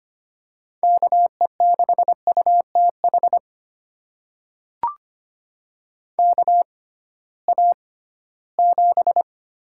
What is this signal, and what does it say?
Morse code, heard as a keyed tone, about 25 wpm, spaced as for 16 wpm, 700 Hz. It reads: KE6UTH KA7
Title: USA Call Signs - 6 Characters - 6x Character Spacing - 25wpm